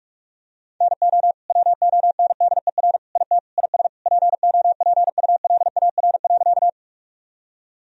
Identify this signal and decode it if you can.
Morse code, heard as a keyed tone, about 48 wpm, 700 Hz. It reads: NO WONDER IT IS POPULAR.
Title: Sentences from Top 1000 Words - Review Part 3 - 45wpm